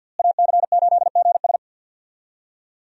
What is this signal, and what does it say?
Morse code, heard as a keyed tone, about 46 wpm, 700 Hz. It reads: AC8GS
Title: USA Call Signs - 5 Characters - Repeated 2x - 50wpm